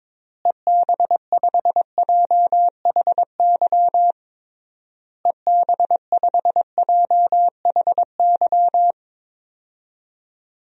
Morse code, heard as a keyed tone, about 22 wpm, 700 Hz. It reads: EB5JHY EB5JHY
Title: DXCC-100 Call Signs - 6 Characters - Repeated 3x - 22wpm